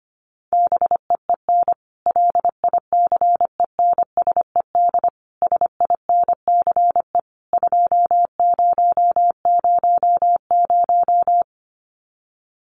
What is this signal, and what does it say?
Morse code, heard as a keyed tone, about 25 wpm, 700 Hz. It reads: BEEN LICENSED SINCE 2000
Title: Common QSO Phrases 25wpm